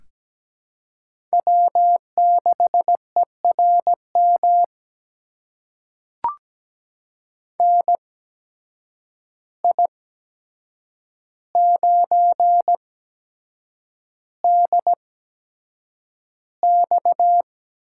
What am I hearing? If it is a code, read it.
Morse code, heard as a keyed tone, about 17 wpm, spaced as for 7 wpm, 700 Hz. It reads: W6ERM NI9DX
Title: USA Call Signs - 8x Character Spacing - 17wpm